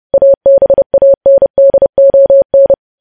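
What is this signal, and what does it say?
Morse code, heard as a keyed tone, about 30 wpm, 550 Hz. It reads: ABANDON
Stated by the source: Morse news headlines sent at 30.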